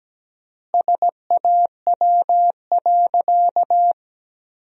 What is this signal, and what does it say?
Morse code, heard as a keyed tone, about 17 wpm, 700 Hz. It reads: SAW.
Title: 3x Word-Spacing - Sentences from Top 300 Words - 17wpm